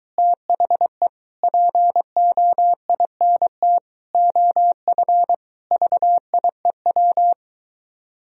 Morse code, heard as a keyed tone, about 23 wpm, 700 Hz. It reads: THE POINT OF VIEW